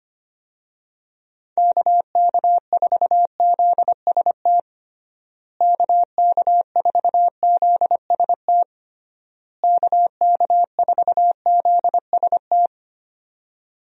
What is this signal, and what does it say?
Morse code, heard as a keyed tone, about 25 wpm, 700 Hz. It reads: KK4ZST KK4ZST KK4ZST
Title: USA Call Signs - 6 Characters - Repeated 3x - 25wpm